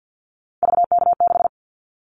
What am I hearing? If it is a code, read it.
Morse code, heard as a keyed tone, about 50 wpm, 700 Hz. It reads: VK6
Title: Callsign Prefixes for Top 100 DXCC - 50wpm